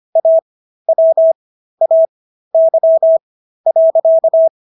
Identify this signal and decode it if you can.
Morse code, heard as a keyed tone, about 25 wpm, spaced as for 13 wpm, 650 Hz. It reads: AWAY.